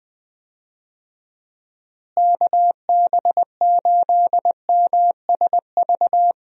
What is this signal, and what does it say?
Morse code, heard as a keyed tone, about 20 wpm, 700 Hz. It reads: KB8MSV
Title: USA Call Signs - 6 Characters - Repeated 2x - 20wpm